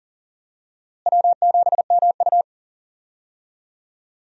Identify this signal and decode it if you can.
Morse code, heard as a keyed tone, about 40 wpm, 700 Hz. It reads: W7MU
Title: USA Call Signs - 4 Characters - Repeated 2x - 40wpm